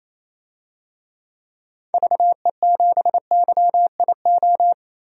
Morse code, heard as a keyed tone, about 28 wpm, 700 Hz. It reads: VE7YIO
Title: DXCC-100 Call Signs - 6 Characters - 28wpm